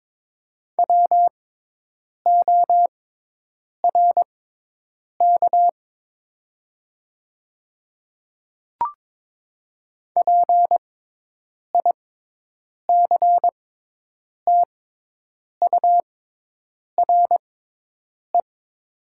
Morse code, heard as a keyed tone, about 22 wpm, spaced as for 8 wpm, 700 Hz. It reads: WORK PICTURE
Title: 6x Character Spacing - Top 200 Words - 22wpm